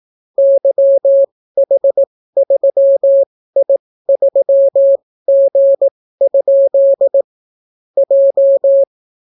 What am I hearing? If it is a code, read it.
Morse code, heard as a keyed tone, about 18 wpm, spaced as for 14 wpm, 550 Hz. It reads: YH3I3G? J